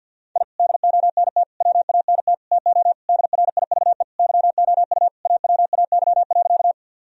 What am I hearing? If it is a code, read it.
Morse code, heard as a keyed tone, about 50 wpm, 700 Hz. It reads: I DONT WANT TO DRIVE YOU AWAY.
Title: Sentences from Top 1000 Words - Review Part 2 - 50wpm